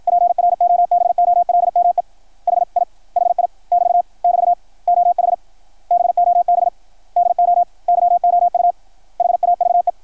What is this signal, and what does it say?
Morse code, heard as a keyed tone, about 53 wpm, 690 Hz. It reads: PROBOLME HI HI <BT> <BT> OH BOB DO YOU HAVE
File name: archived / CRSnet